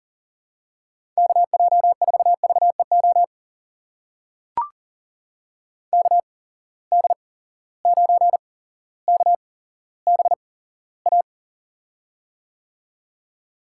Morse code, heard as a keyed tone, about 40 wpm, spaced as for 17 wpm, 700 Hz. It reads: KJ4VEO KD9KBA
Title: USA Call Signs - 8x Character Spacing - 40wpm